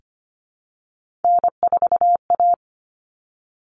Morse code, heard as a keyed tone, about 25 wpm, 700 Hz. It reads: N4A
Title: All Cut Numbers - 3-Digit 25wpm